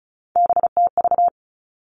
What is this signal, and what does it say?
Morse code, heard as a keyed tone, about 36 wpm, 700 Hz. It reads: BTV
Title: Three Challenging Letter-Number Combinations 35wpm